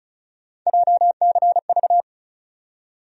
Morse code, heard as a keyed tone, about 34 wpm, 700 Hz. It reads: JCV
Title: USA Call Sign Postfixes - 3 Letters 35wpm